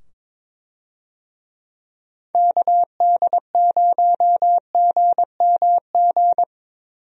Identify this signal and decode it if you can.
Morse code, heard as a keyed tone, about 22 wpm, 700 Hz. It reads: KD0GMG
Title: USA Call Signs - 6 Characters - 22wpm